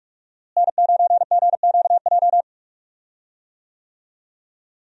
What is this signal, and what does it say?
Morse code, heard as a keyed tone, about 42 wpm, 700 Hz. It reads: N9GQJ
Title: USA Call Signs - 5 Characters - Repeated 3x - 45wpm